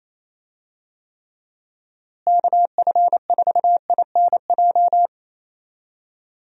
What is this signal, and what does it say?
Morse code, heard as a keyed tone, about 28 wpm, 700 Hz. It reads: KF4INJ
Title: USA Call Signs - 6 Characters - Repeated 3x - 28wpm